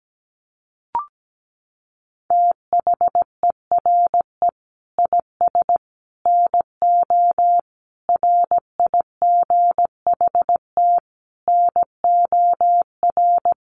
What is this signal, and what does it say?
Morse code, heard as a keyed tone, about 17 wpm, 700 Hz. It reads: THERE IS NO RIGHT NOR
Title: Sentences from Top 1500 Words 17wpm